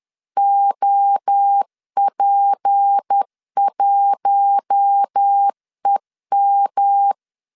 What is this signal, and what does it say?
Morse code, heard as a keyed tone, about 10 wpm, 800 Hz. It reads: OP1EM
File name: LEZIONI / 6 PKX0 / 6 P 10wpm 2432